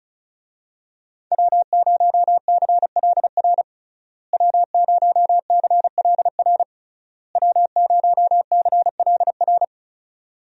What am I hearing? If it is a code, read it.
Morse code, heard as a keyed tone, about 34 wpm, 700 Hz. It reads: W0CLR W0CLR W0CLR
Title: USA Call Signs - 5 Characters - Repeated 3x - 35wpm